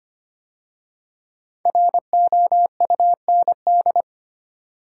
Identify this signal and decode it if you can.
Morse code, heard as a keyed tone, about 25 wpm, 700 Hz. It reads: ROUND
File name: 5 Letter Words - First 3 Letters Given - 25wpm